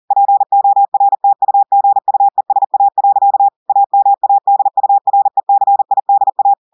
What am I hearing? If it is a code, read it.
Morse code, heard as a keyed tone, about 40 wpm, 800 Hz. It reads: PORTUGUESA. AMADURECIDA
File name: audios / portugais / portugais texte en clair 1 vitesse 40 mots minutes